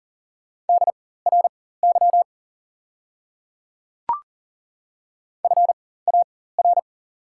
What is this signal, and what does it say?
Morse code, heard as a keyed tone, about 40 wpm, spaced as for 19 wpm, 690 Hz. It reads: DRY FAR
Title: 4x Character Spacing - Top 500 Words - 40wpm